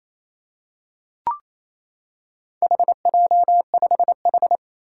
Morse code, heard as a keyed tone, about 28 wpm, 700 Hz. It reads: HJ5H
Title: Four Challenging Letter-Number Combinations 28wpm